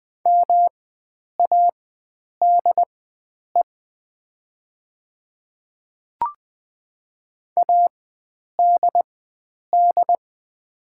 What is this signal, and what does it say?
Morse code, heard as a keyed tone, about 20 wpm, spaced as for 9 wpm, 700 Hz. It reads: MADE ADD
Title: 4x Character Spacing - Top 500 Words - 20wpm